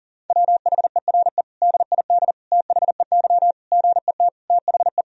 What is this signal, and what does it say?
Morse code, heard as a keyed tone, about 40 wpm, 700 Hz. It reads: WHERE DID THEY GET THE
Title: Sentences from Top 400 Words and Abbreviations 40wpm